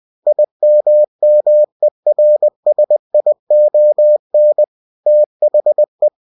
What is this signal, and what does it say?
Morse code, heard as a keyed tone, about 20 wpm, 600 Hz. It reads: IMMERSION THE
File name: winnie-20wpm